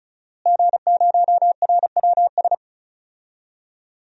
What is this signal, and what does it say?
Morse code, heard as a keyed tone, about 36 wpm, 700 Hz. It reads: G0RWS
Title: DXCC-100 Call Signs - 5 Characters - Repeated 2x - 35wpm